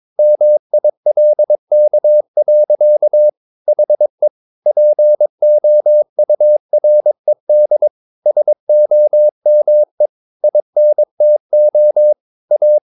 Morse code, heard as a keyed tone, about 22 wpm, 600 Hz. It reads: MILK. HE POURED SOME INTO A
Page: Romanen Never af Ken Follet på engelsk.